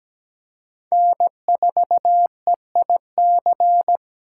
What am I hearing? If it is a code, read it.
Morse code, heard as a keyed tone, about 17 wpm, 700 Hz. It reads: N4EIC
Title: USA Call Signs - 5 Characters - 2x Character Spacing - 17wpm